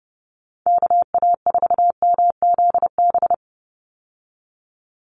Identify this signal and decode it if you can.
Morse code, heard as a keyed tone, about 29 wpm, 700 Hz. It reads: KA4MZB
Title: USA Call Signs 30wpm